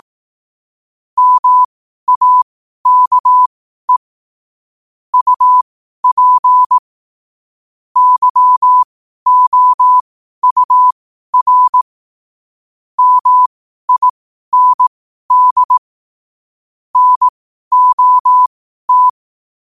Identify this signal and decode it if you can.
Morse code, heard as a keyed tone, about 18 wpm, spaced as for 12 wpm, 1000 Hz. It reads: MAKE UP YOUR MIND NOT
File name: qotdfiles / 12wpm